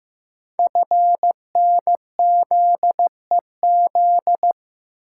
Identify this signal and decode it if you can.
Morse code, heard as a keyed tone, about 15 wpm, 700 Hz. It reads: FNZEZ
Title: Code Groups - 5 Letters - 15wpm